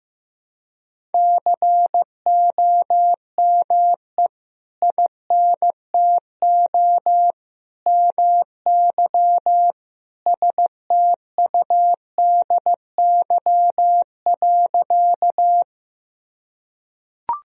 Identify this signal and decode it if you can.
Morse code, heard as a keyed tone, about 16 wpm, 700 Hz. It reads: COME INTO MY STUDY.
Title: Sentences from Top 1000 Words - Review Part 1 - 15wpm